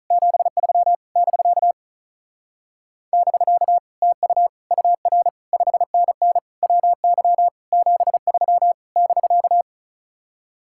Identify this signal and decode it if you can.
Morse code, heard as a keyed tone, about 44 wpm, 700 Hz. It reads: 73 <BK> <BK> TU UR 5NN WY 73 <BK>